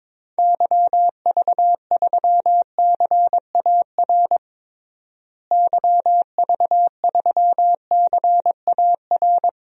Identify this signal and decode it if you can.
Morse code, heard as a keyed tone, about 22 wpm, 700 Hz. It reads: YV3CAR YV3CAR
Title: DXCC-100 Call Signs - 6 Characters - Repeated 2x - 22wpm